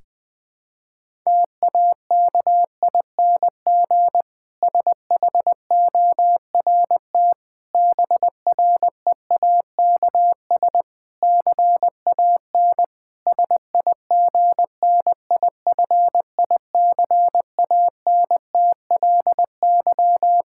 Morse code, heard as a keyed tone, about 20 wpm, 700 Hz. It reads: TAKING SHORT BREAKS CAN SIGNIFICANTLY
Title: Difficult Sentences with Hints - 20wpm